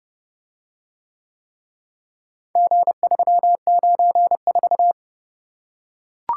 Morse code, heard as a keyed tone, about 28 wpm, 700 Hz. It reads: G394